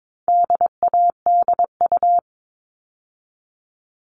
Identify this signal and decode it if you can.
Morse code, heard as a keyed tone, about 22 wpm, 700 Hz. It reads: DADU
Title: All Cut Numbers - 4-Digit 22wpm